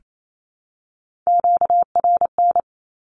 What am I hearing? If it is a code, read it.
Morse code, heard as a keyed tone, about 28 wpm, 700 Hz. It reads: QRN
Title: Q-Codes Repeated 3x 28wpm